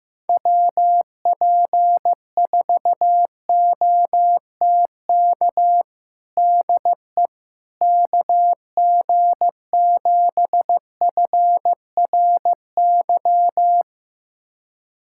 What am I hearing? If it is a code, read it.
Morse code, heard as a keyed tone, about 15 wpm, 700 Hz. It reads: WP4OTK DE KG7FRY
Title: USA Callsign DE Callsign - 6 Characters - 15wpm